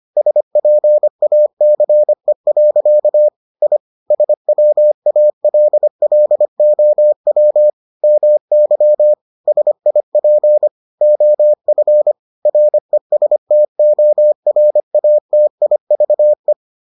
Morse code, heard as a keyed tone, about 25 wpm, 600 Hz. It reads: SPACE. I SWALLOW MY SIP OF RESTORATIVE